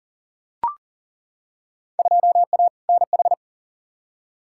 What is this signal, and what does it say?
Morse code, heard as a keyed tone, about 40 wpm, 700 Hz. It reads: 2A NH
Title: Field Day Exchanges - 40wpm